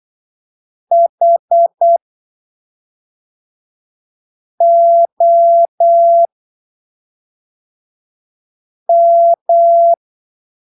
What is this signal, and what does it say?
Morse code, heard as a keyed tone, about 8 wpm, spaced as for 3 wpm, 680 Hz. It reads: HOM
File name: arquivos / texto / texto 23 08ppm